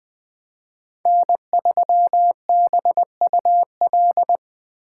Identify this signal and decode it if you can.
Morse code, heard as a keyed tone, about 20 wpm, 700 Hz. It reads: N3BUL
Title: USA Call Signs - 5 Characters - 4x Character Spacing - 20wpm